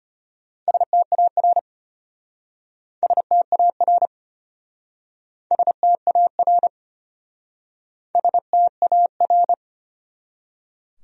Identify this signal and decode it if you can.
Morse code, heard as a keyed tone, about 33 wpm, 700 Hz. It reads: STAR STAR STAR STAR